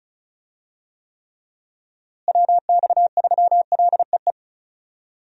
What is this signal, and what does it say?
Morse code, heard as a keyed tone, about 36 wpm, 700 Hz. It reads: WX3LEE
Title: USA Call Signs - 6 Characters - Repeated 2x - 35wpm